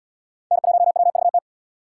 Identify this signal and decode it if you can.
Morse code, heard as a keyed tone, about 77 wpm, 700 Hz. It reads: N1WVT